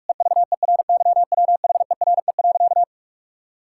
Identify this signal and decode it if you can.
Morse code, heard as a keyed tone, about 41 wpm, 700 Hz. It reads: EVERYWHERE.
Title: Sentences from Top 1600 Words 45wpm